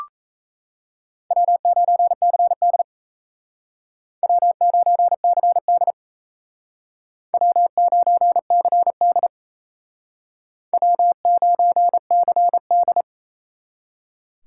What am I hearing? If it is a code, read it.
Morse code, heard as a keyed tone, about 38 wpm, 700 Hz. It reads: W9CD W9CD W9CD W9CD